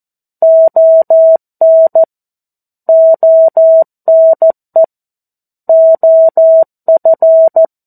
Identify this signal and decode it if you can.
Morse code, heard as a keyed tone, about 14 wpm, 650 Hz. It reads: ON ONE OF
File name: 02-Chapter-02 14 WPM